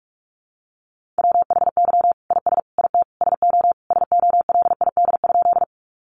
Morse code, heard as a keyed tone, about 44 wpm, 700 Hz. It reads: WHY IS IT SO SOLID?
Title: Sentences from Top 1500 Words 45wpm